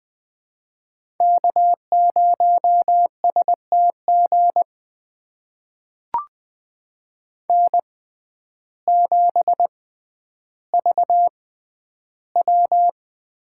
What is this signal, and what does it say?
Morse code, heard as a keyed tone, about 20 wpm, spaced as for 11 wpm, 700 Hz. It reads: K0STG N7VW
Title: USA Call Signs - 5 Characters - 6x Character Spacing - 20wpm